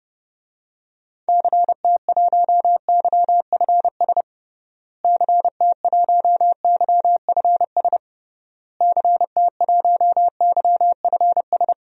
Morse code, heard as a keyed tone, about 29 wpm, 700 Hz. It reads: CT1YFS CT1YFS CT1YFS
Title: DXCC-100 Call Signs - 6 Characters - Repeated 3x - 30wpm